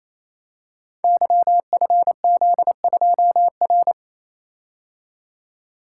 Morse code, heard as a keyed tone, about 29 wpm, 700 Hz. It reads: YFZ2R
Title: Code Groups - 5 Letter:Numbers - 28wpm